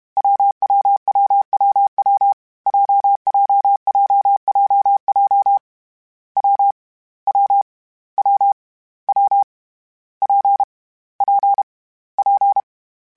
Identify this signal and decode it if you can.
Morse code, heard as a keyed tone, about 32 wpm, 800 Hz. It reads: WWWWW JJJJJ W W W W P P P
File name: Variable Speed / Variable Speed-mp3s / WPJ WPJ1 sp @30